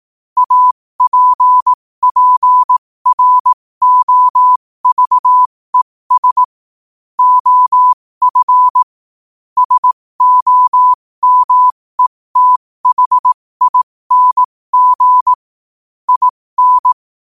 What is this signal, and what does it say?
Morse code, heard as a keyed tone, about 18 wpm, spaced as for 15 wpm, 1000 Hz. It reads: APPROVES OF SOMETHING IN
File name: qotdfiles / 15wpm